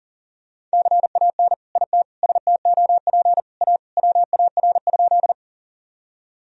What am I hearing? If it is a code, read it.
Morse code, heard as a keyed tone, about 37 wpm, 700 Hz. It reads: CAN IT STOP A WAR?
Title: Sentences from Top 1000 Words - Review Part 2 - 40wpm